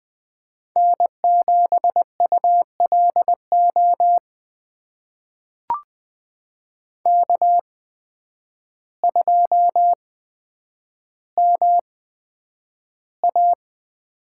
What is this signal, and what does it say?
Morse code, heard as a keyed tone, about 20 wpm, spaced as for 9 wpm, 700 Hz. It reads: N7ULO K2MA
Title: USA Call Signs - 5 Characters - 8x Character Spacing - 20wpm